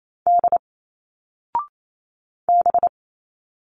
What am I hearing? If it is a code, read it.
Morse code, heard as a keyed tone, about 28 wpm, 700 Hz. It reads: D B
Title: Single Letters 28wpm